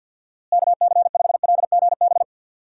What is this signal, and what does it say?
Morse code, heard as a keyed tone, about 46 wpm, 700 Hz. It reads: KC5LGB
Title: USA Call Signs - 6 Characters - Repeated 3x - 50wpm